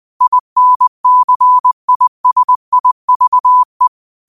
Morse code, heard as a keyed tone, about 20 wpm, 1000 Hz. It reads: INCISIVE
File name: qotdfiles / 20wpm